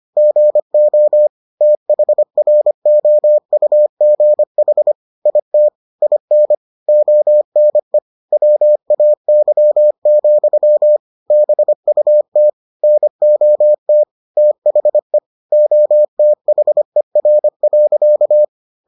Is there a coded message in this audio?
Yes: Morse code from 0.2 to 18.4 s, heard as a keyed tone, about 25 wpm, 600 Hz. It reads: GO THROUGH IT IN ONE WAY, BUT NOT THE OTHER.